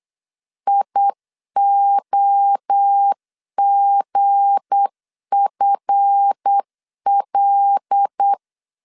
Morse code, heard as a keyed tone, about 8 wpm, 800 Hz. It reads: IOGFL
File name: LEZIONI / 7 GF9Z / 7 GF 8wpm 3108